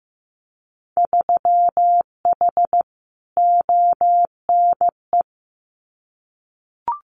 Morse code, heard as a keyed tone, about 15 wpm, 700 Hz. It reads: 3H ONE
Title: Winter Field Day Exchanges Encore 15wpm